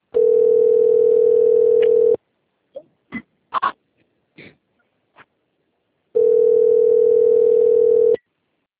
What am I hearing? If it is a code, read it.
Ringback tone (North America), 440 + 480 Hz.